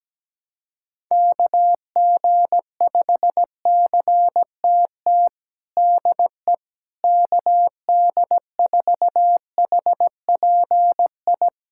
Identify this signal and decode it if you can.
Morse code, heard as a keyed tone, about 17 wpm, 700 Hz. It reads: KG5CTT DE KD4HPI
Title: USA Callsign DE Callsign 17wpm